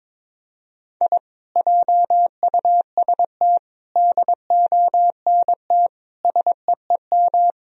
Morse code, heard as a keyed tone, about 21 wpm, 700 Hz. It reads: I JUST DONT SEEM
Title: Sentences from Top 1000 Words - Review Part 2 - 22wpm